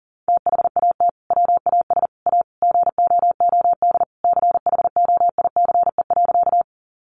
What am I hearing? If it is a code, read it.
Morse code, heard as a keyed tone, about 40 wpm, 700 Hz. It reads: THAT WAS A GOOD CHOICE.
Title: Sentences from Top 900 Words 40wpm